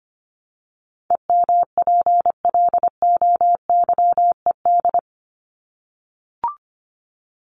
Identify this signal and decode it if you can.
Morse code, heard as a keyed tone, about 25 wpm, 700 Hz. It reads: EMPLOYED
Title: ED Suffix - Top 100 Words with Base Words 25wpm